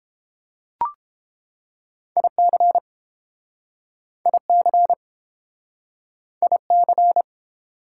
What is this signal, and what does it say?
Morse code, heard as a keyed tone, about 30 wpm, 700 Hz. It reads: IC IC IC